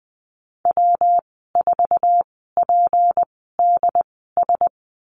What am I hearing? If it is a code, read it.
Morse code, heard as a keyed tone, about 20 wpm, spaced as for 15 wpm, 700 Hz. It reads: W4PDS
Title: 193 Lesson 31g - Letter v - Call Signs 20wpm